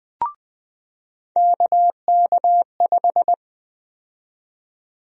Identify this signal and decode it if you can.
Morse code, heard as a keyed tone, about 20 wpm, 700 Hz. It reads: KK5
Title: USA Call Sign Prefixes - 20wpm